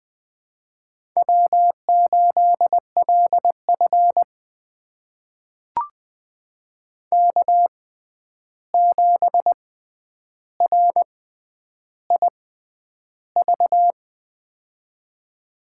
Morse code, heard as a keyed tone, about 20 wpm, spaced as for 10 wpm, 700 Hz. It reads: W8LF K7RIV
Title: USA Call Signs - 6x Character Spacing - 20wpm